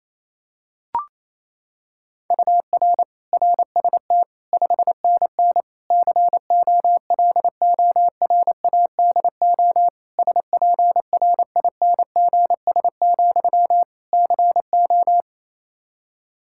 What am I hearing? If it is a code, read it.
Morse code, heard as a keyed tone, about 28 wpm, 700 Hz. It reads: UR RST 5NN COLORADO SPRINGS, CO
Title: Practice QSO - Exchange 1 - Beginning - 28wpm